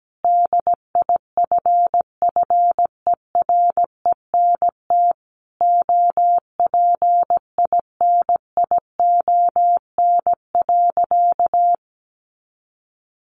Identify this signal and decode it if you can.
Morse code, heard as a keyed tone, about 17 wpm, 700 Hz. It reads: DIFFERENT OPINION.
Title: Sentences from Top 1200 Words 17wpm